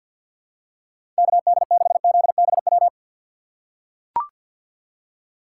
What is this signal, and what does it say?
Morse code, heard as a keyed tone, about 45 wpm, 700 Hz. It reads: KD6ZBW
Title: USA Call Signs - 2x Character Spacing - 50wpm